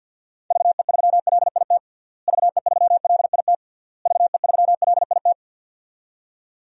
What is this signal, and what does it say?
Morse code, heard as a keyed tone, about 46 wpm, 700 Hz. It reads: VE3LIT VE3LIT VE3LIT
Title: DXCC-100 Call Signs - 6 Characters - Repeated 3x - 50wpm